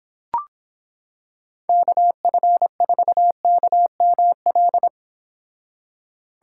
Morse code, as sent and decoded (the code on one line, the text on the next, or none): -.- ..-. ....- -.- -- .-..
KF4KML